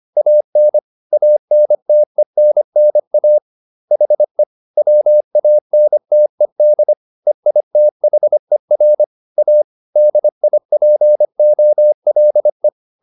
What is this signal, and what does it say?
Morse code, heard as a keyed tone, about 25 wpm, 600 Hz. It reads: AN ANTENNA HE WANTED EITHER A DIPOLE
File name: SS202 25